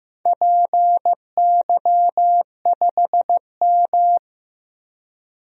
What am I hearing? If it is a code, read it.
Morse code, heard as a keyed tone, about 15 wpm, 700 Hz. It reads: PY5M
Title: DXCC-100 Call Signs - 4 Characters - 15wpm